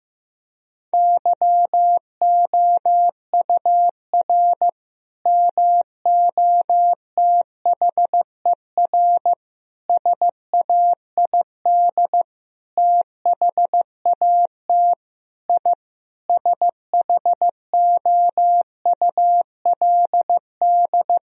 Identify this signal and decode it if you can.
Morse code, heard as a keyed tone, about 15 wpm, 700 Hz. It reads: YOUR MOTHER SAID THAT I SHOULD
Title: Sentences from Top 1000 Words - Review Part 2 - 15wpm